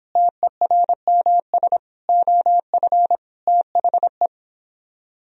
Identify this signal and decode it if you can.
Morse code, heard as a keyed tone, about 31 wpm, 700 Hz. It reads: TERMS OF THE